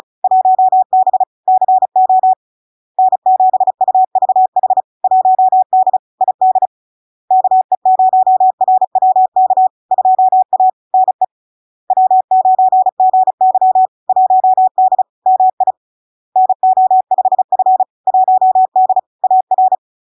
Morse code, heard as a keyed tone, about 36 wpm, 750 Hz. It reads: 1B CO N7UVH 1D ID KE0RWK 2A NE W9GY 1D MI NO5F 1D AR
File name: LARK-FD-2022-Log-in-CW-35wpm-750-Hz